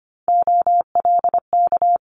Morse code, heard as a keyed tone, about 25 wpm, 700 Hz. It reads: OLK
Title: Multi-letter Phonemes Quick Fire 25wpm